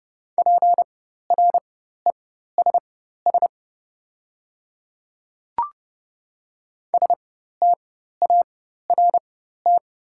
Morse code, heard as a keyed tone, about 30 wpm, spaced as for 14 wpm, 700 Hz. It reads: PRESS START
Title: 4x Character Spacing - Top 300 Words - 30wpm